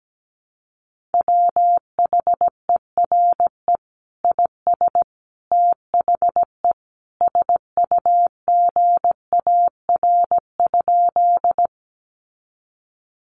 Morse code, heard as a keyed tone, about 17 wpm, 700 Hz. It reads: WHERE IS THE SUGAR?
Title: Sentences from Top 1200 Words 17wpm